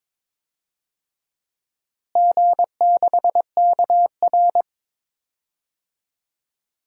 Morse code, heard as a keyed tone, about 21 wpm, 700 Hz. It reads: G6KR
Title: DXCC-100 Call Signs - 4 Characters - Repeated 2x - 22wpm